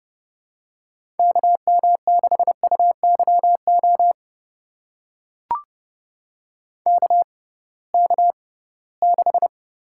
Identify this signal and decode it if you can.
Morse code, heard as a keyed tone, about 31 wpm, spaced as for 20 wpm, 700 Hz. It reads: KM6UYO KK6
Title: USA Call Signs - 6 Characters - 6x Character Spacing - 30wpm